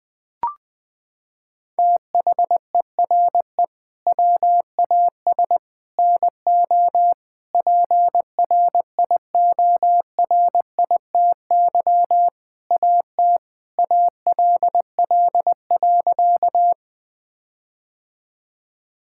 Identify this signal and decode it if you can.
Morse code, heard as a keyed tone, about 20 wpm, 700 Hz. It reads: THERE WAS NO PRIORITY AT ALL.
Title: Sentences from Top 1600 Words 20wpm